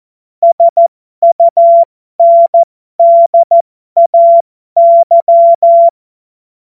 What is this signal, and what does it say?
Morse code, heard as a keyed tone, about 14 wpm, spaced as for 12 wpm, 680 Hz. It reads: SUNDAY
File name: Morsekurs / Witze